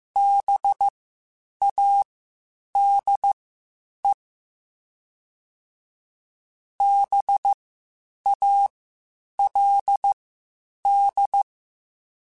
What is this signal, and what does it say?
Morse code, heard as a keyed tone, about 15 wpm, spaced as for 8 wpm, 800 Hz. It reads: BADE BALD
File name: Spaced Alphas / Spaced Alphas mp3s / Bs sp @13